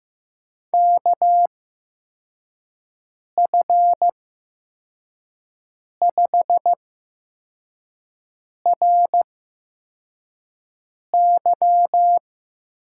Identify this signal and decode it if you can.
Morse code, heard as a keyed tone, about 15 wpm, spaced as for 4 wpm, 700 Hz. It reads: KF5RY
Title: USA Call Signs - 6 Characters - 8x Character Spacing - 15wpm